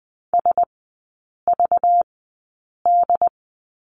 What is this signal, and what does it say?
Morse code, heard as a keyed tone, about 20 wpm, spaced as for 8 wpm, 690 Hz. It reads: SVD
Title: 205 Lesson 33e - Letter j - Three Characters 20wpm